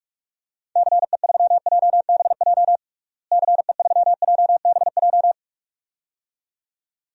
Morse code, heard as a keyed tone, about 46 wpm, 700 Hz. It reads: CE3JBJ CE3JBJ
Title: DXCC-100 Call Signs - 6 Characters - Repeated 2x - 45wpm